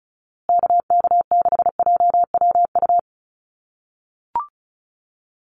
Morse code, heard as a keyed tone, about 34 wpm, 700 Hz. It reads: KK6JWU
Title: USA Call Signs Repeated 3x 35wpm